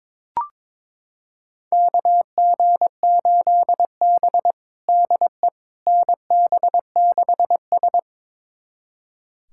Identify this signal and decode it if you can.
Morse code, heard as a keyed tone, about 21 wpm, 700 Hz. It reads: KG8B DE NB6S
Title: USA Callsign DE Callsign - 4 Characters - 22wpm